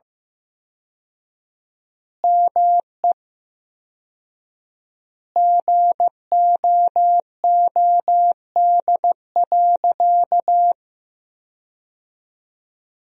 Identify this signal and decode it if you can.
Morse code, heard as a keyed tone, about 15 wpm, 700 Hz. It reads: ME GOOD.
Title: 4x Word-Spacing - Sentences from Top 500 Words - 15wpm